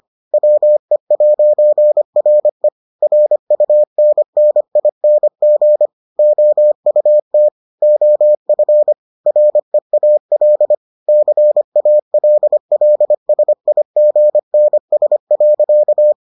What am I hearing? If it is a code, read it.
Morse code, heard as a keyed tone, about 25 wpm, 600 Hz. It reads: WE'RE RUNNING OUT OF REAL CALLSIGNS.